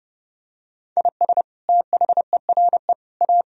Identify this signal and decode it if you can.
Morse code, heard as a keyed tone, about 30 wpm, 700 Hz. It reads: IS THERE A
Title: Sentences from Top 1000 Words - Review Part 4 - 30wpm